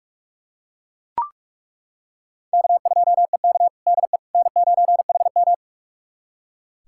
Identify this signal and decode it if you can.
Morse code, heard as a keyed tone, about 43 wpm, 700 Hz. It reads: K2EK DE N9HM
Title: USA Callsign DE Callsign - 4 Characters - 45wpm